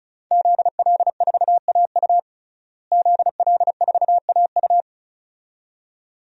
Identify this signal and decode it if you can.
Morse code, heard as a keyed tone, about 36 wpm, 700 Hz. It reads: ZL4AU ZL4AU
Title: DXCC-100 Call Signs - 5 Characters - Repeated 2x - 35wpm